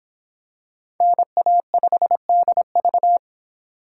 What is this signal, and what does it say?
Morse code, heard as a keyed tone, about 25 wpm, 700 Hz. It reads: NA5DV